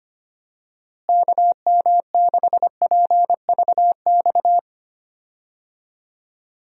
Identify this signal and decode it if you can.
Morse code, heard as a keyed tone, about 25 wpm, 700 Hz. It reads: KM6PVX
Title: USA Call Signs - 6 Characters - Repeated 3x - 25wpm